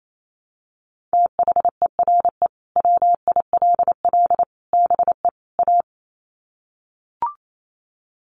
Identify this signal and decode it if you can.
Morse code, heard as a keyed tone, about 28 wpm, 700 Hz. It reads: THERE WILL BE A
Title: Sets of 4 Words 28wpm